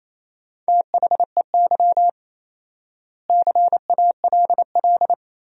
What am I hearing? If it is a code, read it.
Morse code, heard as a keyed tone, about 28 wpm, 700 Hz. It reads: THEY CALL
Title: 4x Word-Spacing - Sentences from Top 500 Words - 28wpm